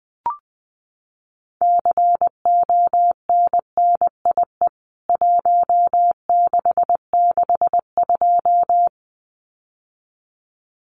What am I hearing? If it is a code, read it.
Morse code, heard as a keyed tone, about 20 wpm, 700 Hz. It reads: CONNIE 1662
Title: CWT Exchange 20wpm